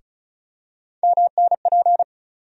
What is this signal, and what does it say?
Morse code, heard as a keyed tone, about 34 wpm, 700 Hz. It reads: MNP
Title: USA Call Sign Postfixes - 3 Letters 35wpm